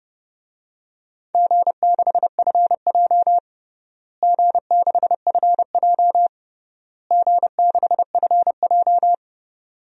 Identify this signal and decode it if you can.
Morse code, heard as a keyed tone, about 30 wpm, 700 Hz. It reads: G6FJ G6FJ G6FJ
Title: DXCC-100 Call Signs - 4 Characters - Repeated 3x - 30wpm